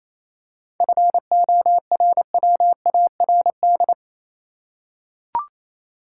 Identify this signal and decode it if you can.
Morse code, heard as a keyed tone, about 28 wpm, 700 Hz. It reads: FORWARD
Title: Seven Letter Words - 28wpm